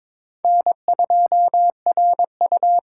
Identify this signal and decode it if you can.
Morse code, heard as a keyed tone, about 26 wpm, 700 Hz. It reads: N2RU
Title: Speed Racing - USA 1x2 Call Signs 22wpm